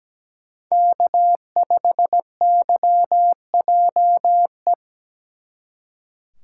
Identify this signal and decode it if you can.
Morse code, heard as a keyed tone, about 20 wpm, 700 Hz. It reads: K5YJE
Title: Speed Racing - USA 1x3 Call Signs 17wpm